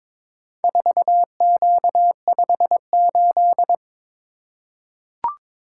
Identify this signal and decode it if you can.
Morse code, heard as a keyed tone, about 22 wpm, 700 Hz. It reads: 4Q58
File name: practice / 22wpm / Four Letter-Number Combinations - 22wpm